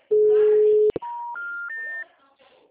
Special information tone.